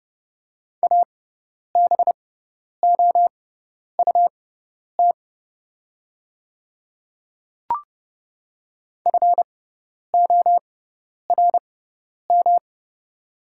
Morse code, heard as a keyed tone, about 30 wpm, spaced as for 10 wpm, 700 Hz. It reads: ABOUT FORM
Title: 6x Character Spacing - Top 300 Words - 30wpm